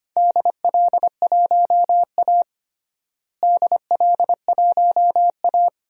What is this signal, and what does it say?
Morse code, heard as a keyed tone, about 25 wpm, 700 Hz. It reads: DL1A DL1A
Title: DXCC-100 Call Signs - 4 Characters - Repeated 3x - 25wpm